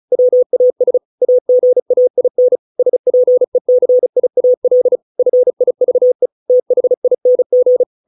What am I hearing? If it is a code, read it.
Morse code, heard as a keyed tone, about 34 wpm, 500 Hz. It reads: WAS AGAIN SPECIAL FIVE THING